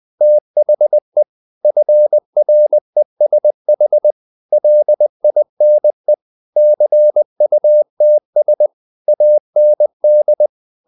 Morse code, heard as a keyed tone, about 20 wpm, 600 Hz. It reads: THE FRESH LINE CUTS AND
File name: the old man and the sea part 2 20wpm